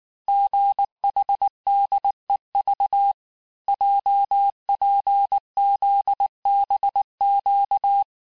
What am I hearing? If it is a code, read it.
Morse code, heard as a keyed tone, about 19 wpm, 800 Hz. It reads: GHDEV JPZBQ
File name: cours-du-lundi files / cours-1595-teste-CHIFFRE-vitesse18-MPM